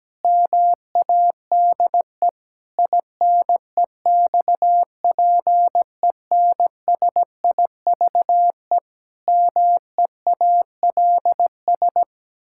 Morse code, heard as a keyed tone, about 17 wpm, 700 Hz. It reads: MADE INEXPENSIVE MEALS
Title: Pangram Challenge v2 - 17wpm